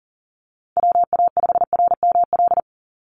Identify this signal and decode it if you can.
Morse code, heard as a keyed tone, about 40 wpm, 700 Hz. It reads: WA5RML
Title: Call Signs - 40wpm